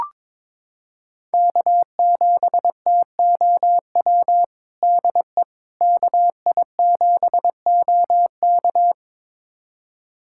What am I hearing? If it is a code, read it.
Morse code, heard as a keyed tone, about 21 wpm, 700 Hz. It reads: K7TOW DE KI7OK
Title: USA Callsign DE Callsign - 5 Characters - 22wpm